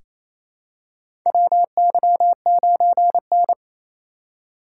Morse code, heard as a keyed tone, about 28 wpm, 700 Hz. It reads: WY9N
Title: USA Call Signs - 2x Character Spacing - 28wpm